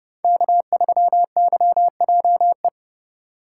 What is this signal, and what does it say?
Morse code, heard as a keyed tone, about 30 wpm, 700 Hz. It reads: K3YJE
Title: USA Call Signs - 5 Characters - 2x Character Spacing - 30wpm